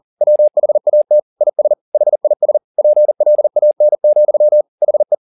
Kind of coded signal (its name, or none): Morse code